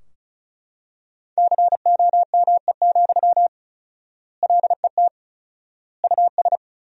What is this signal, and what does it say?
Morse code, heard as a keyed tone, about 36 wpm, 700 Hz. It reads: COME, LET US
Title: 4x Word-Spacing - Sentences from Top 300 Words - 35wpm